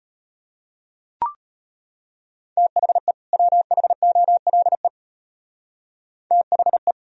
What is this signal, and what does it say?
Morse code, heard as a keyed tone, about 36 wpm, 700 Hz. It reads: THE WHOLE THE